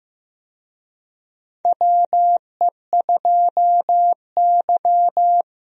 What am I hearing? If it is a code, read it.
Morse code, heard as a keyed tone, about 15 wpm, 700 Hz. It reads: WE2Y
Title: USA Call Signs - 4 Characters - Repeated 2x - 15wpm